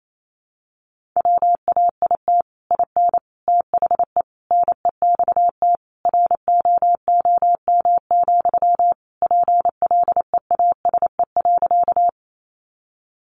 Morse code, heard as a keyed tone, about 28 wpm, 700 Hz. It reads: WAIT IN THE NEXT ROOM, PLEASE.
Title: Sentences from Top 1500 Words 28wpm